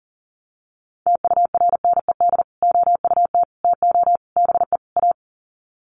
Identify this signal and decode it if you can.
Morse code, heard as a keyed tone, about 40 wpm, 700 Hz. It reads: TURNED OUT TO BE A
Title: Sets of 5 Words 40wpm